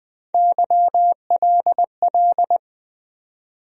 Morse code, heard as a keyed tone, about 20 wpm, 700 Hz. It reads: YLL
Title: USA Call Sign Postfixes - 3 Letters 20wpm